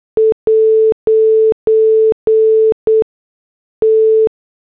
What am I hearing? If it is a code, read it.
Morse code, heard as a keyed tone, about 8 wpm, spaced as for 6 wpm, 440 Hz. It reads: 'T
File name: morse / headlines